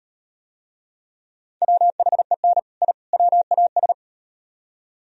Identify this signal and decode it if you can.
Morse code, heard as a keyed tone, about 38 wpm, 700 Hz. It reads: WHEN I WAS